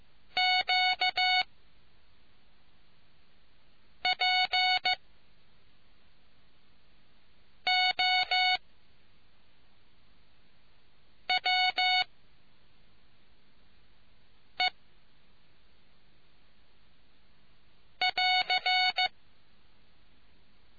Morse code, heard as a keyed tone, about 15 wpm, spaced as for 3 wpm, 740 Hz. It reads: QPOWE<AR>